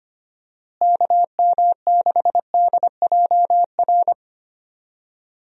Morse code, heard as a keyed tone, about 25 wpm, 700 Hz. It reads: KM6DJR
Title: USA Call Signs - 6 Characters - Repeated 3x - 25wpm